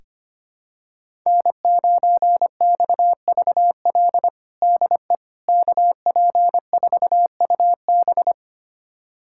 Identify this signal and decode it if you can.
Morse code, heard as a keyed tone, about 25 wpm, 700 Hz. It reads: N9XVL DE KP4UB
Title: USA Callsign DE Callsign 25wpm